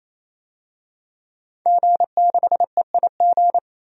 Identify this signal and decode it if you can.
Morse code, heard as a keyed tone, about 28 wpm, 700 Hz. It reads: G6EIG
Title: DXCC-100 Call Signs - 5 Characters - Repeated 3x - 28wpm